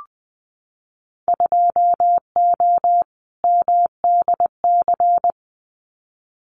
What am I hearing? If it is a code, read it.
Morse code, heard as a keyed tone, about 20 wpm, 700 Hz. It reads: 2O MDC
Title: Winter Field Day Exchanges 20wpm